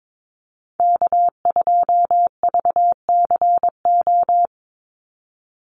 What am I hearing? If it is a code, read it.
Morse code, heard as a keyed tone, about 22 wpm, 700 Hz. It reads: K2VCO
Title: Call Signs - 22wpm